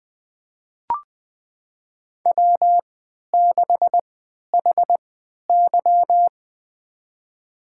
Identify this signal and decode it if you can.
Morse code, heard as a keyed tone, about 20 wpm, spaced as for 11 wpm, 700 Hz. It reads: W6HY
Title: USA Call Signs - 4 Characters - 3x Character Spacing - 20wpm